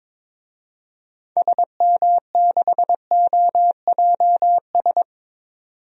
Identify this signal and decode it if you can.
Morse code, heard as a keyed tone, about 22 wpm, 700 Hz. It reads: SM6OJS
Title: DXCC-100 Call Signs - 6 Characters - 22wpm